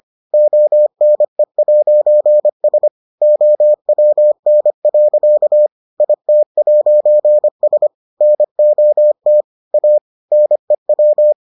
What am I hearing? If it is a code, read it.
Morse code, heard as a keyed tone, about 25 wpm, 600 Hz. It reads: ONE'S OWN. IT'S NOT A NEW